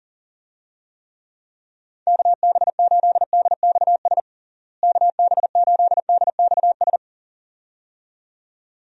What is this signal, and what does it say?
Morse code, heard as a keyed tone, about 40 wpm, 700 Hz. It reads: KB8DXS KB8DXS
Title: USA Call Signs - 6 Characters - Repeated 2x - 40wpm